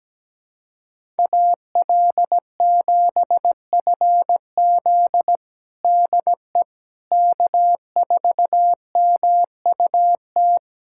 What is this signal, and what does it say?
Morse code, heard as a keyed tone, about 17 wpm, 700 Hz. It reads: AL7FZ DE K4MUT
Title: USA Callsign DE Callsign - 5 Characters - 17wpm